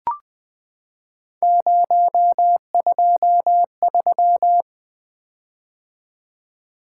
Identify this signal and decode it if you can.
Morse code, heard as a keyed tone, about 20 wpm, 700 Hz. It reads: 023
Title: Three Letter-Number Combinations - 20wpm